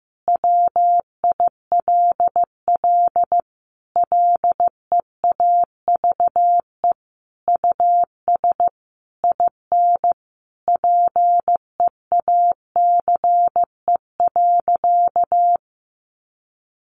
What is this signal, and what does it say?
Morse code, heard as a keyed tone, about 15 wpm, 700 Hz. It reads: WILL LEAVE US IN PEACE.
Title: Sentences from Top 1500 Words 15wpm